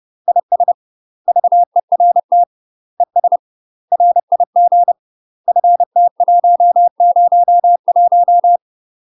Morse code, heard as a keyed tone, about 30 wpm, 700 Hz. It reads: IS VERT ES RIG FT101
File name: UK MASTER QSOs TRAINING / 30WPM / UK MASTER QSOs 0027